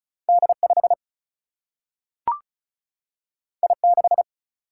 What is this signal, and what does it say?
Morse code, heard as a keyed tone, about 36 wpm, 700 Hz. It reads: D5 I6
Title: Single-Letter Single-Number 35wpm